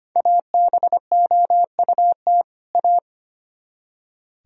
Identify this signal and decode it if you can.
Morse code, heard as a keyed tone, about 32 wpm, 700 Hz. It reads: ABOUT A